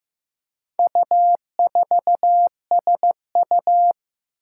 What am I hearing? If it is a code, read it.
Morse code, heard as a keyed tone, about 15 wpm, 700 Hz. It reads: U4SU
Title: Four Challenging Letter-Number Combinations 15wpm